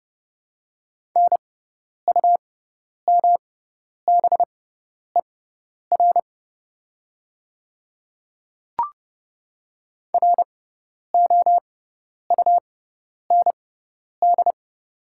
Morse code, heard as a keyed tone, about 30 wpm, spaced as for 10 wpm, 700 Hz. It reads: NUMBER ROUND
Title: 6x Character Spacing - Top 200 Words - 30wpm